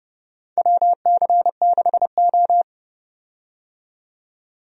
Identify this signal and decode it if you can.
Morse code, heard as a keyed tone, about 30 wpm, 700 Hz. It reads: WC6O
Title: USA Call Signs - 4 Characters - 30wpm